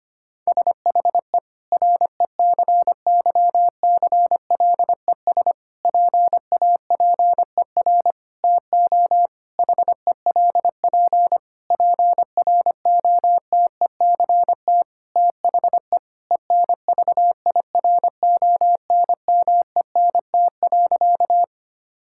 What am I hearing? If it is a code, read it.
Morse code, heard as a keyed tone, about 25 wpm, 700 Hz. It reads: SHE RECYCLES PAPER TO HELP PROTECT THE ENVIRONMENT.
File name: practice / 25wpm / Difficult Sentences with Hints - 25wpm